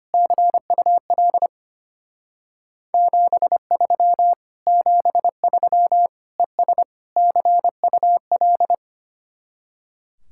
Morse code, heard as a keyed tone, about 30 wpm, 700 Hz. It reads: CUL 73 73 ES CUL
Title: Speed-Racing Common QSO Phrases - 25wpm